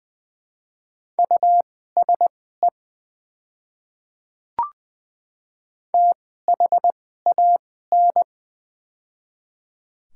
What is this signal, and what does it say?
Morse code, heard as a keyed tone, about 20 wpm, spaced as for 14 wpm, 700 Hz. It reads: USE THAN
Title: 2x Character Spacing - Top 500 Words - 20wpm